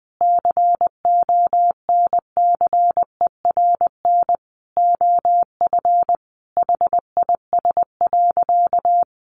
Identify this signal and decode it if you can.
Morse code, heard as a keyed tone, about 20 wpm, 700 Hz. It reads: CONCERN OF HIS.
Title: Sentences from Top 1000 Words 20wpm